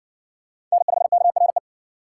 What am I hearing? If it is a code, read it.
Morse code, heard as a keyed tone, about 52 wpm, 700 Hz. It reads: N5KRE